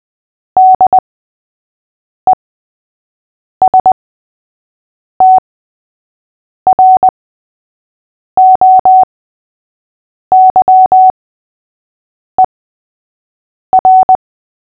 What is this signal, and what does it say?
Morse code, heard as a keyed tone, about 20 wpm, spaced as for 6 wpm, 750 Hz. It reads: DESTROYER
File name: cw / fists / Short-Story-3-6wpm